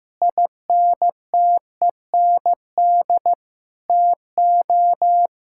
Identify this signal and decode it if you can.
Morse code, heard as a keyed tone, about 15 wpm, 700 Hz. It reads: INTEND TO
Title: Sentences from Top 1600 Words 15wpm